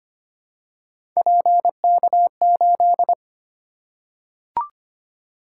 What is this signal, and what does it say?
Morse code, heard as a keyed tone, about 25 wpm, 700 Hz. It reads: PK8
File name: Three Letter-Number Combinations - 25wpm